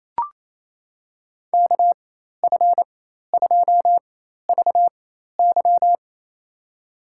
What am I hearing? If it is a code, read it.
Morse code, heard as a keyed tone, about 28 wpm, spaced as for 13 wpm, 700 Hz. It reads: KF2VY
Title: USA Call Signs - 5 Characters - 4x Character Spacing - 28wpm